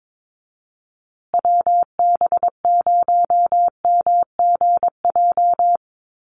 Morse code, heard as a keyed tone, about 22 wpm, 700 Hz. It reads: WB0MGJ
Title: USA Call Signs Repeated 2x 22wpm